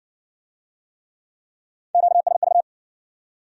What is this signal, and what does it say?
Morse code, heard as a keyed tone, about 56 wpm, 700 Hz. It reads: QSV